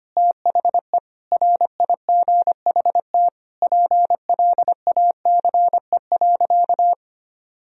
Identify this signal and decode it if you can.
Morse code, heard as a keyed tone, about 25 wpm, 700 Hz. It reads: THE RIGHT PLACE.
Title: Sentences from Top 1000 Words - Review Part 3 - 25wpm